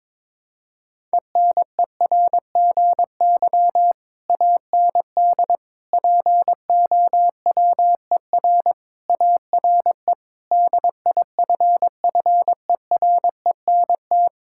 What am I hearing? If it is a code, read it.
Morse code, heard as a keyed tone, about 22 wpm, 700 Hz. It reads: ENERGY AND POWER ARE DIFFERENT
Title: Sentences from Top 1000 Words - Review Part 6 - 22wpm